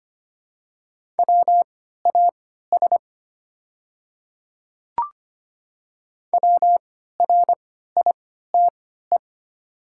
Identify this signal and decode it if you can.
Morse code, heard as a keyed tone, about 25 wpm, spaced as for 14 wpm, 700 Hz. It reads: WAS WRITE
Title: 3x Character Spacing - Top 100 Words - 25wpm